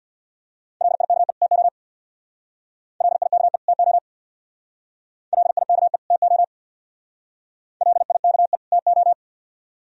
Morse code, heard as a keyed tone, about 68 wpm, 700 Hz. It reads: LIKE TO LIKE TO LIKE TO LIKE TO